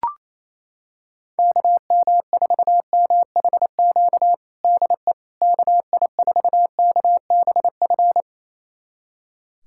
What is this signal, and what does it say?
Morse code, heard as a keyed tone, about 28 wpm, 700 Hz. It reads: KM4MHQ DE KI4KBF
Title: USA Callsign DE Callsign - 6 Characters - 28wpm